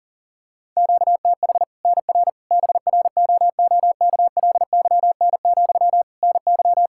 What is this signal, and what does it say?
Morse code, heard as a keyed tone, about 40 wpm, 700 Hz. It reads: QTH NR BROOKLYN, NY
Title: Practice QSO - Exchange 1 - Beginning - 40wpm